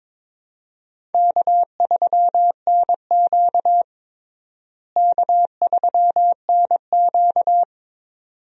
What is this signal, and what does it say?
Morse code, heard as a keyed tone, about 22 wpm, 700 Hz. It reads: K3NQ K3NQ
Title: USA Call Signs - 4 Characters - Repeated 2x - 22wpm